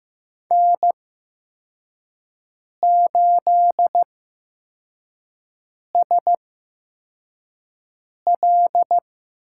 Morse code, heard as a keyed tone, about 15 wpm, spaced as for 4 wpm, 700 Hz. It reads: N8SL
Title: USA Call Signs - 5 Characters - 8x Character Spacing - 15wpm